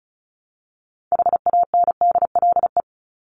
Morse code, heard as a keyed tone, about 34 wpm, 700 Hz. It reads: HANDLE
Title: Top 1001-1250 Words 35wpm